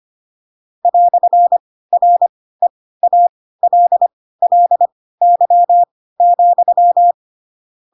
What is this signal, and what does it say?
Morse code, heard as a keyed tone, about 25 wpm, spaced as for 16 wpm, 700 Hz. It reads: "REALLY,
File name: Pinocchio / Pinocchio ch7 10 25 ik1hge